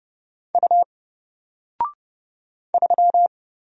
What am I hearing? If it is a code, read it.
Morse code, heard as a keyed tone, about 30 wpm, 700 Hz. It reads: U 3
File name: Single-Letter-Number-30wpm